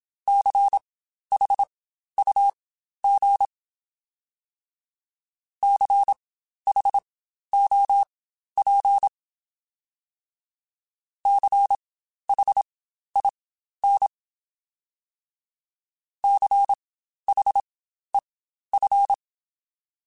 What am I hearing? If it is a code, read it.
Morse code, heard as a keyed tone, about 27 wpm, spaced as for 12 wpm, 800 Hz. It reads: CHUG CHOP CHIN CHEF